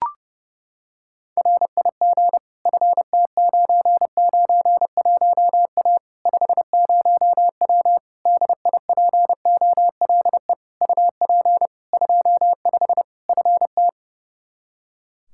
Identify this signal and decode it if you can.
Morse code, heard as a keyed tone, about 30 wpm, 700 Hz. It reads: RIG FT991A 50W DIPOLE UP 25 FT
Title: Practice QSO - Exchange 2 - Info - 30wpm